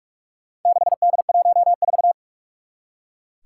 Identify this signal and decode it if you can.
Morse code, heard as a keyed tone, about 44 wpm, 700 Hz. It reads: 6D14
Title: Four Challenging Letter-Number Combinations 45wpm